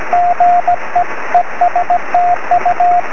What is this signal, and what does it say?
Morse code, heard as a keyed tone, about 17 wpm, 720 Hz, with radio noise. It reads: GE ESTU
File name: sm7lcb / 20 23cm / 2134 dk6aj 23cm